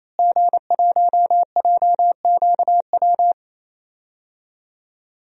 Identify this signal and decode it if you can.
Morse code, heard as a keyed tone, about 28 wpm, 700 Hz. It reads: G1JQW
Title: DXCC-100 Call Signs - 5 Characters - Repeated 3x - 28wpm